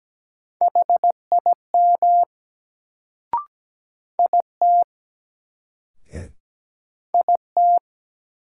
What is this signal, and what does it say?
Morse code, heard as a keyed tone, about 17 wpm, 700 Hz. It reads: HIM IT IT
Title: Sets of 2 Words - New Word Drill - 17wpm